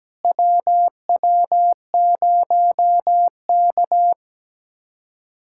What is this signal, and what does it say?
Morse code, heard as a keyed tone, about 17 wpm, 700 Hz. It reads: WW0K
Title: USA Callsign DE Callsign - 4 Characters - 17wpm